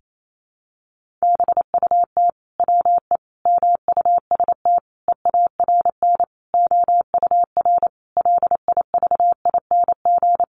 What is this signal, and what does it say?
Morse code, heard as a keyed tone, about 28 wpm, 700 Hz. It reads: BUT WE MUST EARN OUR LIVING
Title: Sentences from Top 1500 Words 28wpm